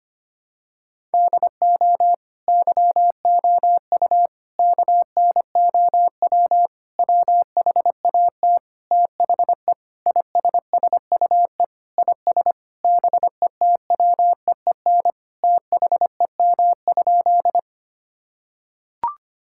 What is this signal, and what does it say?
Morse code, heard as a keyed tone, about 25 wpm, 700 Hz. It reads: DO YOU KNOW WHAT THE ISSUE IS BETWEEN THEM?
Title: Sentences from Top 1000 Words - Review Part 6 - 25wpm